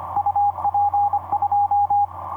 Morse code, heard as a keyed tone, about 25 wpm, 820 Hz, with radio noise. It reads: UP 2
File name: j6-k8eab dl5zb etc 30m cw 06dec2010 2022